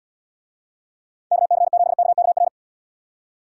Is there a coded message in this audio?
Yes: Morse code from 1.3 to 2.5 s, heard as a keyed tone, about 78 wpm, 700 Hz. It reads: KC7KKR